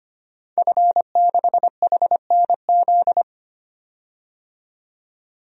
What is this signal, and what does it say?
Morse code, heard as a keyed tone, about 25 wpm, 700 Hz. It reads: F6HNZ
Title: DXCC-100 Call Signs - 5 Characters - 25wpm